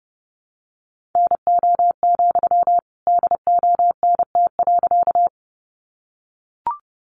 Morse code, heard as a keyed tone, about 30 wpm, 700 Hz. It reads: NO, DONT.
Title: Sentences from Top 300 Words 30wpm